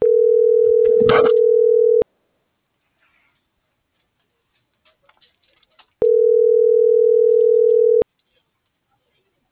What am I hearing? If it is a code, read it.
Ringback tone (North America), 440 + 480 Hz.